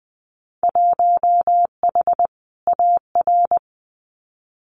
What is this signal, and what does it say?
Morse code, heard as a keyed tone, about 20 wpm, 700 Hz. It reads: 1H AR
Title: Winter Field Day Exchanges 20wpm